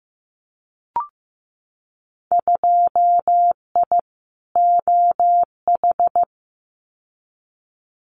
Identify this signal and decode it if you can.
Morse code, heard as a keyed tone, about 15 wpm, 700 Hz. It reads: 2I OH
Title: Winter Field Day Exchanges 15wpm